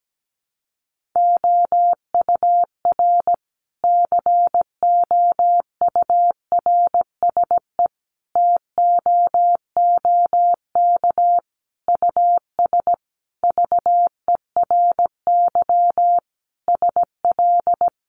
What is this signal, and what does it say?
Morse code, heard as a keyed tone, about 17 wpm, 700 Hz. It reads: OUR COURSE TOOK US VERY SL
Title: Sentences from Top 1300 Words 17wpm